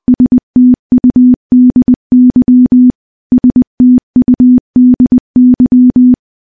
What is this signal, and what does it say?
Morse code, heard as a keyed tone, about 20 wpm, 250 Hz. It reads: STUDY STUDY
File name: morse-training-20WPM